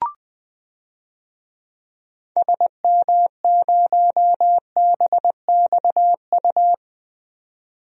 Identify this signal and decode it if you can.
Morse code, heard as a keyed tone, about 21 wpm, 700 Hz. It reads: SM0BXU
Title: DXCC-100 Call Signs - 6 Characters - 20wpm